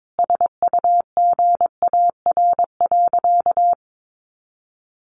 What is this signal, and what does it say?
Morse code, heard as a keyed tone, about 22 wpm, 700 Hz. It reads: SUGAR.
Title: Sentences from Top 1200 Words 22wpm